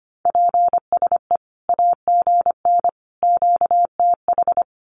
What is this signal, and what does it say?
Morse code, heard as a keyed tone, about 25 wpm, 700 Hz. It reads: PSE AGN QTH
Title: Common QSO Phrases 25wpm